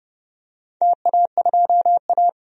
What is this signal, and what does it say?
Morse code, heard as a keyed tone, about 29 wpm, 700 Hz. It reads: TA2A
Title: DXCC-100 Call Signs - 4 Characters - 30wpm